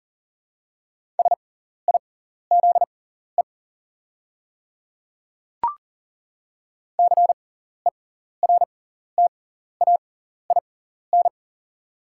Morse code, heard as a keyed tone, about 40 wpm, spaced as for 14 wpm, 690 Hz. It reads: SIZE CERTAIN
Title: 6x Character Spacing - Top 500 Words - 40wpm